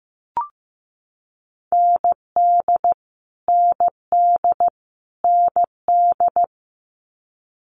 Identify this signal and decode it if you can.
Morse code, heard as a keyed tone, about 15 wpm, 700 Hz. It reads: ND ND ND
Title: Multi-letter Phonemes Repeated 3x 15wpm